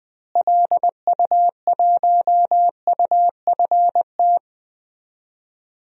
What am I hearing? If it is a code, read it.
Morse code, heard as a keyed tone, about 20 wpm, 700 Hz. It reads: LU1UFT
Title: DXCC-100 Call Signs - 6 Characters - Repeated 2x - 20wpm